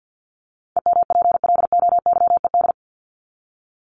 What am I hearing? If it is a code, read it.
Morse code, heard as a keyed tone, about 50 wpm, 700 Hz. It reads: EMPLOYED
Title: ED Suffix - Top 100 Words with Base Words 50wpm